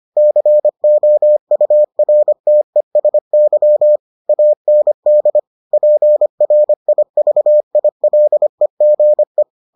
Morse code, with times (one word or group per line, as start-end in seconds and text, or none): 0.2-4.0 COURTESY
4.3-5.4 AND
5.7-9.4 PRIVILEGE